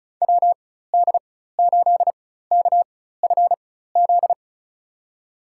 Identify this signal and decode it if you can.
Morse code, heard as a keyed tone, about 36 wpm, spaced as for 16 wpm, 700 Hz. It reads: WD8KFZ
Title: USA Call Signs - 6 Characters - 4x Character Spacing - 35wpm